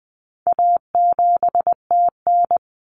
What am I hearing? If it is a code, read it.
Morse code, heard as a keyed tone, about 20 wpm, 700 Hz. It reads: A7TN
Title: Common Cut Numbers - 4-Digit 20wpm